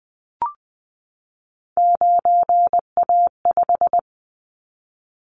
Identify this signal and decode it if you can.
Morse code, heard as a keyed tone, about 20 wpm, 700 Hz. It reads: 9A5
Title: Callsign Prefixes for Top 25 DXCC - 20wpm